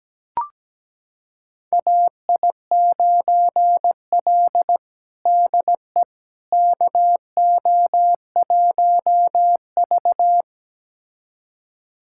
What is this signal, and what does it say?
Morse code, heard as a keyed tone, about 17 wpm, 700 Hz. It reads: AI9L DE KO1V
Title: USA Callsign DE Callsign - 4 Characters - 17wpm